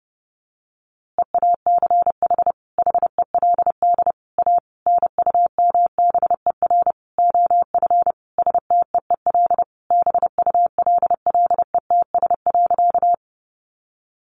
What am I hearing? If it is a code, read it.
Morse code, heard as a keyed tone, about 28 wpm, 700 Hz. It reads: EACH HELD A NUMBER OF STEEL BULLETS.
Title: A Sound Of Thunder 30wpm